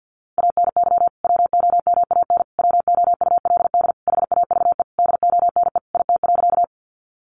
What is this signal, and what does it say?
Morse code, heard as a keyed tone, about 50 wpm, 700 Hz. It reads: ANY WOMAN WOULD HAVE DONE IT.
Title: Sentences from Top 1100 Words 50wpm